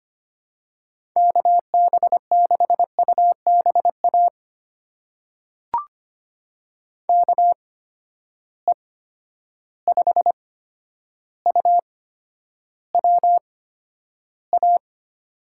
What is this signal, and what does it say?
Morse code, heard as a keyed tone, about 25 wpm, spaced as for 11 wpm, 700 Hz. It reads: KB6UBA KE5UWA
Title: USA Call Signs - 6 Characters - 8x Character Spacing - 25wpm